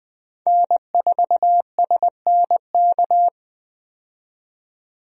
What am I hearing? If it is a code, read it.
Morse code, heard as a keyed tone, about 20 wpm, 700 Hz. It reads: N4SNK
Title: USA Call Signs - 5 Characters - 20wpm